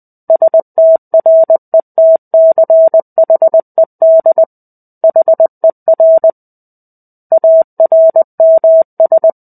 Morse code, heard as a keyed tone, about 20 wpm, 650 Hz. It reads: STRETCHED HER ARMS
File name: 05-Chapter-05 20 WPM